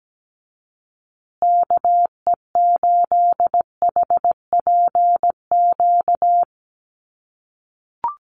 Morse code, heard as a keyed tone, about 17 wpm, 700 Hz. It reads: KE8HPQ
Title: USA Call Signs Repeated 3x 17wpm